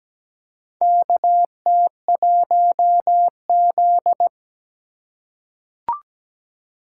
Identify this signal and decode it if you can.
Morse code, heard as a keyed tone, about 17 wpm, 700 Hz. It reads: KT1Z
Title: USA Call Signs - 4 Characters - 2x Character Spacing - 17wpm